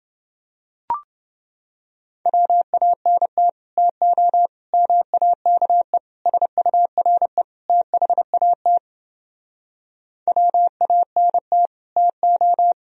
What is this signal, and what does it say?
Morse code, heard as a keyed tone, about 30 wpm, 700 Hz. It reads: WANT TO MAKE SURE THAT WANT TO